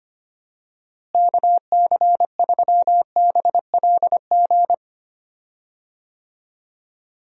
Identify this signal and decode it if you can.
Morse code, heard as a keyed tone, about 25 wpm, 700 Hz. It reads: KC3BLG
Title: USA Call Signs - 6 Characters - Repeated 3x - 25wpm